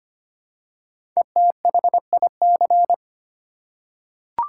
Morse code, heard as a keyed tone, about 26 wpm, 700 Hz. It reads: ETHIC
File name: 5 Letter Words - First 3 Letters Given - 25wpm